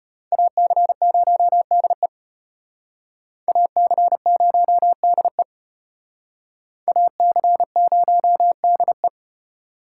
Morse code, heard as a keyed tone, about 33 wpm, 700 Hz. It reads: AC0DE AC0DE AC0DE